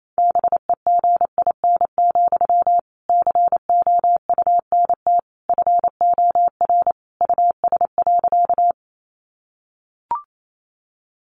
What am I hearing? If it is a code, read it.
Morse code, heard as a keyed tone, about 28 wpm, 700 Hz. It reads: BEGIN, COUNT FOR US.
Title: Sentences from Top 1500 Words 28wpm